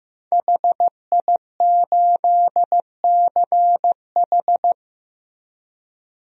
Morse code, heard as a keyed tone, about 15 wpm, 700 Hz. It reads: HI8CH
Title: DXCC-100 Call Signs - 5 Characters - Repeated 3x - 15wpm